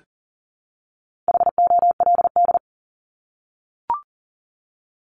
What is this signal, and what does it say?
Morse code, heard as a keyed tone, about 40 wpm, 700 Hz. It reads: HOLD
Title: ING Suffix - Top 100 Words with Base Words 40wpm